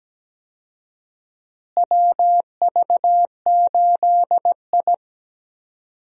Morse code, heard as a keyed tone, about 17 wpm, 700 Hz. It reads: WV8I
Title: USA Call Signs - 4 Characters - Repeated 3x - 17wpm